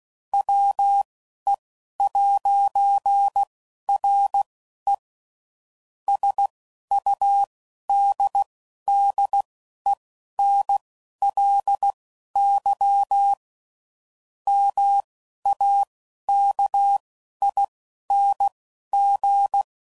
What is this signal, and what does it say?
Morse code, heard as a keyed tone, about 16 wpm, spaced as for 11 wpm, 800 Hz. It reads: WE'RE SUDDENLY MAKING
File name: CW Code Groups @10WPM / Technical Subjects / AND Gate